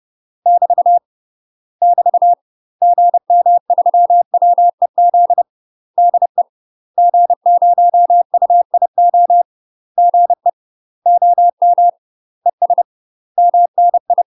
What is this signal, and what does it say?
Morse code, heard as a keyed tone, about 30 wpm, 700 Hz. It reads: <BT> <BT> GM3WEZ DE G0UIO GE OM ES MNI
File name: UK MASTER QSOs TRAINING / 30WPM / UK MASTER QSOs 0005